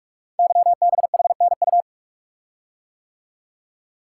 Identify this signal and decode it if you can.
Morse code, heard as a keyed tone, about 44 wpm, 700 Hz. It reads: YBHNU
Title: Code Groups - 5 Letter:Numbers - 45wpm